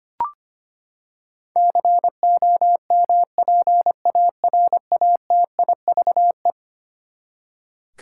Morse code, heard as a keyed tone, about 25 wpm, 700 Hz. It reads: COMPARATIVE
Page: morse-code-ninja-practice-25wpm